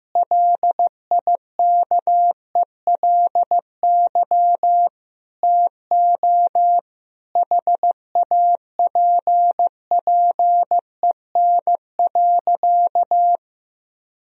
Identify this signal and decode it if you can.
Morse code, heard as a keyed tone, about 15 wpm, 700 Hz. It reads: LIKELY TO HAPPEN.
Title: Sentences from Top 1000 Words - Review Part 4 - 15wpm